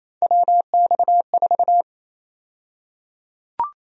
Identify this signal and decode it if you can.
Morse code, heard as a keyed tone, about 27 wpm, 700 Hz. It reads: WX4
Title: USA Call Sign Prefixes - Repeated 2x - 28wpm